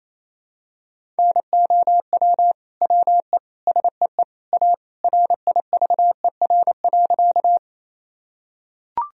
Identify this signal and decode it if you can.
Morse code, heard as a keyed tone, about 28 wpm, 700 Hz. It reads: NOW WE SEE A RIVER.
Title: Sentences from Top 1000 Words - Review Part 1 - 28wpm